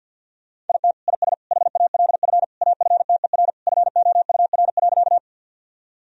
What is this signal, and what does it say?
Morse code, heard as a keyed tone, about 49 wpm, 700 Hz. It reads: IT IS HALF AFTER FOUR.
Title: Sentences from Top 1000 Words - Review Part 1 - 50wpm